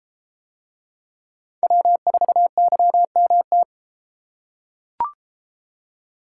Morse code, heard as a keyed tone, about 33 wpm, 700 Hz. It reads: W4YMT